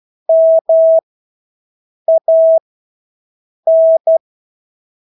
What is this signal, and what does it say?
Morse code, heard as a keyed tone, about 12 wpm, spaced as for 6 wpm, 650 Hz. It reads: MAN
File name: Engelske ord 12-6 wpm